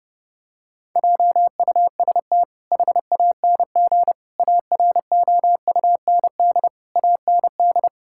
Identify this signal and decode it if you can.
Morse code, heard as a keyed tone, about 32 wpm, 700 Hz. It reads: JUST HANG AROUND AND
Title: Sentences from Top 1600 Words 30wpm